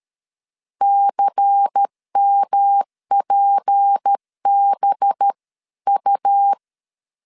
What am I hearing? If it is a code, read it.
Morse code, heard as a keyed tone, about 13 wpm, 800 Hz. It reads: CMPBU
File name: LEZIONI / 8 YJ45 / 8 YJ 12wpm 2235